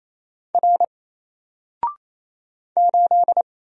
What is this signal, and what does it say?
Morse code, heard as a keyed tone, about 27 wpm, 700 Hz. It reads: R 8
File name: Single-Letter-Number-28wpm